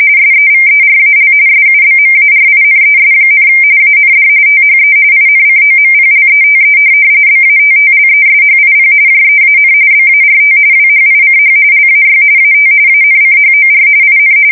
RTTY, about 45.45 baud. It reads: SEVERAL SUBMARINES, AND THE INCOMPLETE BATTLESHIP JEAN BART WHICH LAY AT ANCHOR IN
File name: fsk170 / USS-Massachusetts